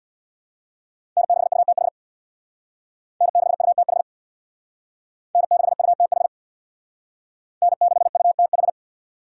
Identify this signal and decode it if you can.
Morse code, heard as a keyed tone, about 61 wpm, 700 Hz. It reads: N6UTH N6UTH N6UTH N6UTH
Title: Speed Racing - USA 1x3 Call Signs 50wpm